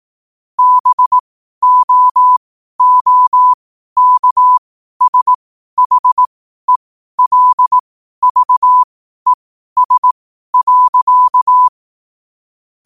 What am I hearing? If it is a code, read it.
Morse code, heard as a keyed tone, about 18 wpm, spaced as for 12 wpm, 1000 Hz. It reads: BOOKSHELVES.